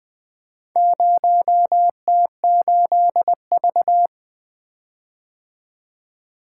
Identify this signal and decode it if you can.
Morse code, heard as a keyed tone, about 20 wpm, 700 Hz. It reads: 0T8V
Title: Four Letter-Number Combinations - 20wpm